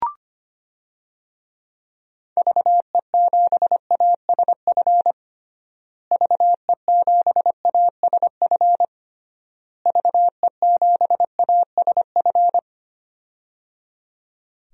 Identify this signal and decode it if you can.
Morse code, heard as a keyed tone, about 25 wpm, 700 Hz. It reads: VE7ASF VE7ASF VE7ASF
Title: DXCC-100 Call Signs - 6 Characters - Repeated 3x - 25wpm